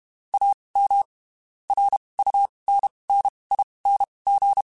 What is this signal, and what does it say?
Morse code, heard as a keyed tone, about 32 wpm, spaced as for 23 wpm, 800 Hz. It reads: AM RUNNING
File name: (spaced) / QSOs Spaced-mp3s / kc7haa @30 sp